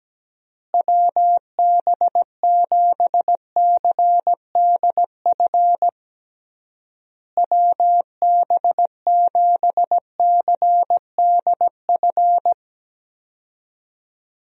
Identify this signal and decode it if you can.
Morse code, heard as a keyed tone, about 17 wpm, 700 Hz. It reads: WB7CDF WB7CDF
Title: USA Call Signs - 6 Characters - Repeated 3x - 17wpm